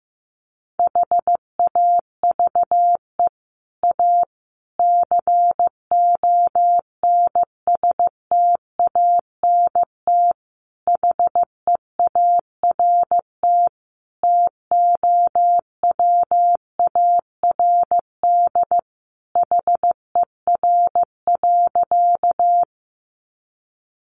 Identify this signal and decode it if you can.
Morse code, heard as a keyed tone, about 15 wpm, 700 Hz. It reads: HAVE A CONSTANT HEART TOWARD HER.
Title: Sentences from Top 1500 Words 15wpm